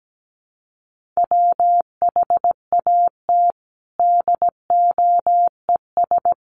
Morse code, heard as a keyed tone, about 17 wpm, 700 Hz. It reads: WHAT DOES
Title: Sentences from Top 1400 Words 17wpm